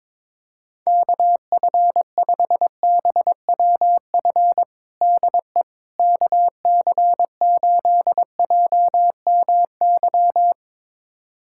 Morse code, heard as a keyed tone, about 22 wpm, 700 Hz. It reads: KF5BWF DE KC8JMY
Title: USA Callsign DE Callsign - 6 Characters - 22wpm